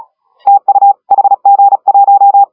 Morse code, heard as a keyed tone, about 34 wpm, 810 Hz. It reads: TU HG1
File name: AUDIO2019 / HG1R CW SSB MIX